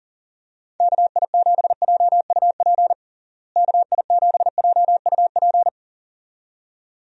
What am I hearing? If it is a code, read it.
Morse code, heard as a keyed tone, about 40 wpm, 700 Hz. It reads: KI7JUP KI7JUP
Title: USA Call Signs - 6 Characters - Repeated 2x - 40wpm